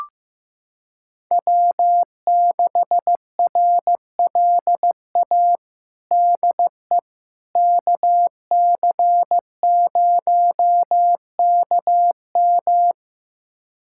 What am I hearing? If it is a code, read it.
Morse code, heard as a keyed tone, about 15 wpm, 700 Hz. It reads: W6RLA DE KC0KM
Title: USA Callsign DE Callsign - 5 Characters - 15wpm